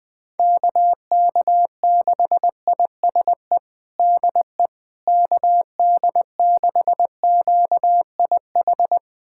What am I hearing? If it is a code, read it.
Morse code, heard as a keyed tone, about 20 wpm, 700 Hz. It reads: KK6ISE DE KD6QIH
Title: USA Callsign DE Callsign - 6 Characters - 20wpm